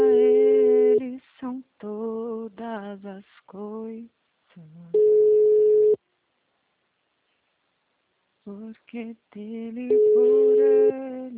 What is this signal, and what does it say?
Ringback tone (Europe, 425 Hz), with speech.
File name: out- -2011- - - .24